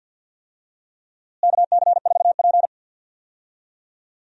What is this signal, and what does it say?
Morse code, heard as a keyed tone, about 53 wpm, 700 Hz. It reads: KC4P
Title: USA Call Signs - 4 Characters - Repeated 2x - 50wpm